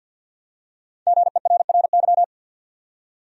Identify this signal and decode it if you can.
Morse code, heard as a keyed tone, about 49 wpm, 700 Hz. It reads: GERRY
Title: USA Common Names - Max 5 Letters - 50wpm